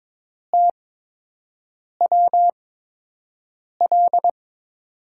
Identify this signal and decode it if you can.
Morse code, heard as a keyed tone, about 22 wpm, spaced as for 6 wpm, 700 Hz. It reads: TWL
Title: USA Call Signs - 6 Characters - 8x Character Spacing - 22wpm